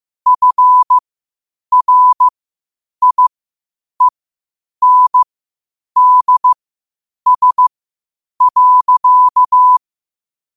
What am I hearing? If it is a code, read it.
Morse code, heard as a keyed tone, about 15 wpm, spaced as for 8 wpm, 1000 Hz. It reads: FRIENDS.
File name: qotdfiles / 8wpm